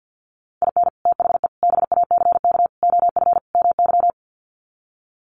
Morse code, heard as a keyed tone, about 50 wpm, 700 Hz. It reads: IN THE BACK OF MY
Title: Sets of 5 Words Encore 50wpm